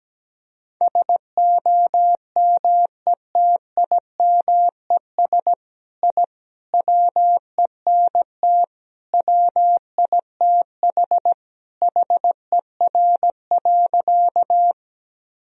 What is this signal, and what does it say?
Morse code, heard as a keyed tone, about 17 wpm, 700 Hz. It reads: SOMETIMES I WENT WITH HER.
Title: Sentences from Top 1000 Words - Review Part 3 - 17wpm